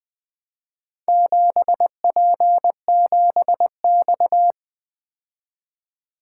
Morse code, heard as a keyed tone, about 24 wpm, 700 Hz. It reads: 7P7X
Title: Speed-Racing - Four Letter-Number Combinations - 20wpm